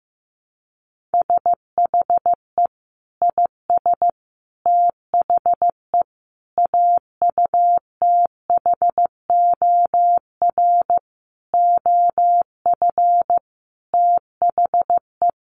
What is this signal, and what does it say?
Morse code, heard as a keyed tone, about 14 wpm, 700 Hz. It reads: SHE IS THE AUTHOR OF THE
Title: Sentences from Top 1200 Words 15wpm